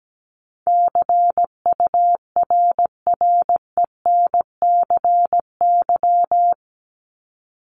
Morse code, heard as a keyed tone, about 17 wpm, 700 Hz. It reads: CURRENCY
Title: Top 1251-1500 Words 17wpm